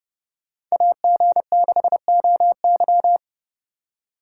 Morse code, heard as a keyed tone, about 38 wpm, 700 Hz. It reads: AG6OY